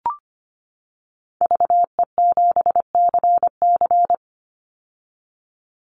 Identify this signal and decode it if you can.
Morse code, heard as a keyed tone, about 25 wpm, 700 Hz. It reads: VE7CC
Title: Call Signs - 25wpm